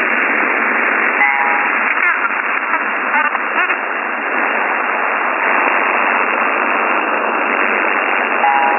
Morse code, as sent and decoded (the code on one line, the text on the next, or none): none